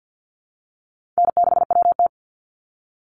Morse code, heard as a keyed tone, about 50 wpm, 700 Hz. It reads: N6WT
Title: NA SOTA Chaser Callsigns 50wpm